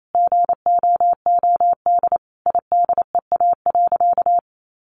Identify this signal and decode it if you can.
Morse code, heard as a keyed tone, about 28 wpm, 700 Hz. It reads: GOOD IDEA.
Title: Sentences from Top 600 Words 28wpm